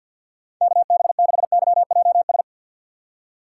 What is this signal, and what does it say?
Morse code, heard as a keyed tone, about 52 wpm, 700 Hz. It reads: KB6YJS
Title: USA Call Signs - 6 Characters - 4x Character Spacing - 50wpm